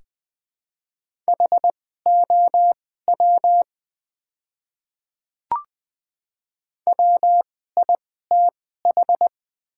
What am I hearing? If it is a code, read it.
Morse code, heard as a keyed tone, about 20 wpm, spaced as for 15 wpm, 700 Hz. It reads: HOW WITH
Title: 2x Character Spacing - Top 300 Words - 20wpm